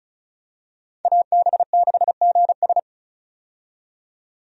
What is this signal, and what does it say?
Morse code, heard as a keyed tone, about 36 wpm, 700 Hz. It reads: AB6GS
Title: USA Call Signs - 5 Characters - 35wpm